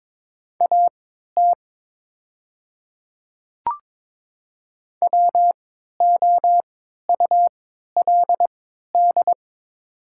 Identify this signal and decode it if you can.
Morse code, heard as a keyed tone, about 22 wpm, spaced as for 13 wpm, 700 Hz. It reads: AT WOULD
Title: 3x Character Spacing - Top 100 Words - 22wpm